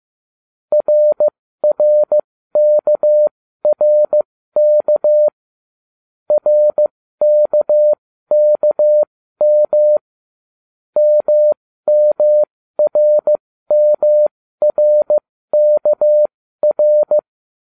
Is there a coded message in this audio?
Yes: Morse code from 0.7 to 17.2 s, heard as a keyed tone, about 15 wpm, spaced as for 13 wpm, 600 Hz. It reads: RRKRK RKKM MMRMRKR